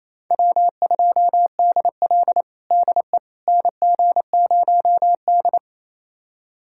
Morse code, heard as a keyed tone, about 28 wpm, 700 Hz. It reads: W2DL DE NG0D
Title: USA Callsign DE Callsign - 4 Characters - 28wpm